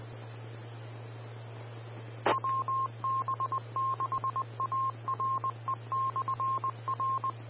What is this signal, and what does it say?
Morse code, heard as a keyed tone, about 20 wpm, 1060 Hz. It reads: WB6ARE/R
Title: Austin, TX Amateur Repeaters